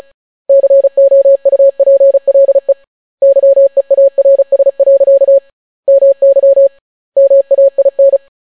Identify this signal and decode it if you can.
Morse code, heard as a keyed tone, about 36 wpm, 550 Hz. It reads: COUPLE YEARS. MY MAIN